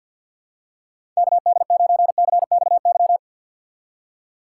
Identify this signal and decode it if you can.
Morse code, heard as a keyed tone, about 48 wpm, 700 Hz. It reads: KD9CXY
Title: USA Call Signs - 6 Characters - 6x Character Spacing - 50wpm